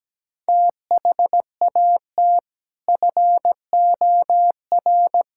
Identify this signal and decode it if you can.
Morse code, heard as a keyed tone, about 17 wpm, 700 Hz. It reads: THAT FOR
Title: Sentences from Top 1000 Words - Review Part 5 - 17wpm